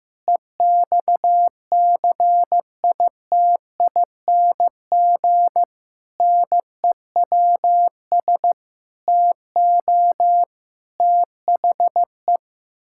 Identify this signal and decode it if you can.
Morse code, heard as a keyed tone, about 15 wpm, 700 Hz. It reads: EXCITING NEWS TO THE
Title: Pangram Challenge v2 - 15wpm